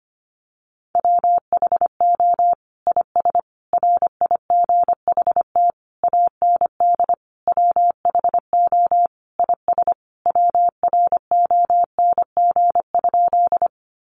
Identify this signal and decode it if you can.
Morse code, heard as a keyed tone, about 25 wpm, 700 Hz. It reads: WHO IS RIGHT AND WHO IS WRONG?
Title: Sentences from Top 900 Words 25wpm